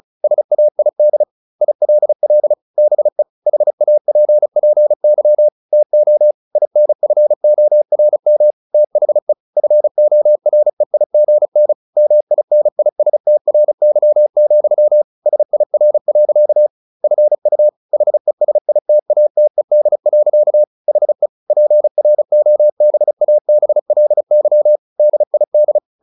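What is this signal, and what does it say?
Morse code, heard as a keyed tone, about 34 wpm, 600 Hz. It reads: SAID ILL BE HAPPY TO INFORM THE FOREIGN MINISTRY, SIR. FU HESITATED. HE PROBABLY DID
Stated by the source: Morse practice at 35 wpm from CW med Gnister, sending Ken Follett's novel Never in English.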